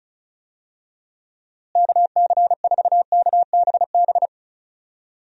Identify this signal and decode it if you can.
Morse code, heard as a keyed tone, about 34 wpm, 700 Hz. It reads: KC4KBB
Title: USA Call Signs - 6 Characters - Repeated 3x - 35wpm